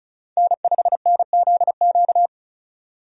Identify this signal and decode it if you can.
Morse code, heard as a keyed tone, about 34 wpm, 700 Hz. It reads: N5NZQ
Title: USA Call Signs - 5 Characters - 35wpm